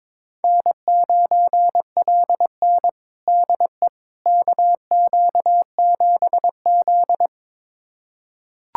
Morse code, heard as a keyed tone, about 22 wpm, 700 Hz. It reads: N9LN DE KQ7Z
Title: USA Callsign DE Callsign - 4 Characters - 22wpm